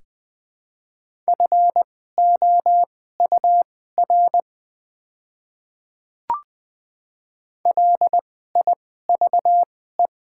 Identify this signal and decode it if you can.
Morse code, heard as a keyed tone, about 20 wpm, spaced as for 14 wpm, 700 Hz. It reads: FOUR LIVE
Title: 2x Character Spacing - Top 500 Words - 20wpm